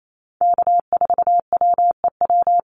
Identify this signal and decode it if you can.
Morse code, heard as a keyed tone, about 28 wpm, 700 Hz. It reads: K4WEW
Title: USA Call Signs Repeated 2x 28wpm